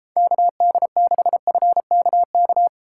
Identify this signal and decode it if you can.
Morse code, heard as a keyed tone, about 32 wpm, 700 Hz. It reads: KD6FKK